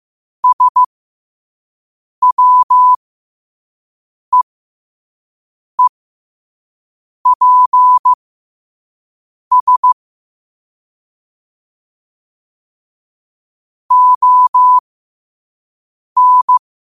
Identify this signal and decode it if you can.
Morse code, heard as a keyed tone, about 15 wpm, spaced as for 5 wpm, 1000 Hz. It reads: SWEEPS ON
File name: qotdfiles / 5wpm